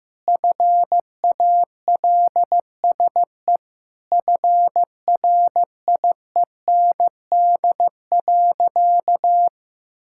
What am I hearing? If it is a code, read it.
Morse code, heard as a keyed tone, about 16 wpm, 700 Hz. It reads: FALSE FRIEND.
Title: Sentences from Top 1600 Words 15wpm